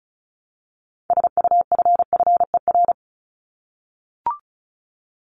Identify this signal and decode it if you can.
Morse code, heard as a keyed tone, about 34 wpm, 700 Hz. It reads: SUFFER
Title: ING Suffix - Top 100 Words with Base Words 35wpm